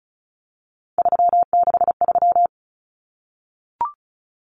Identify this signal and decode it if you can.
Morse code, heard as a keyed tone, about 34 wpm, 700 Hz. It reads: 363
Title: Common Cut Numbers - 3-Digit 35wpm